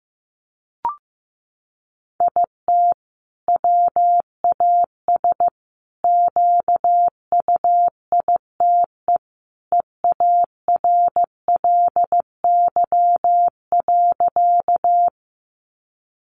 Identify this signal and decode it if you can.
Morse code, heard as a keyed tone, about 15 wpm, 700 Hz. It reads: IT WAS QUITE EARLY.
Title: Sentences from Top 700 Words 15wpm